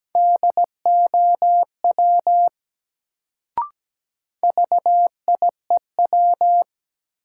Morse code, heard as a keyed tone, about 17 wpm, 700 Hz. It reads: DOW VIEW
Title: Sets of 5 Words - New Word Drill - 17wpm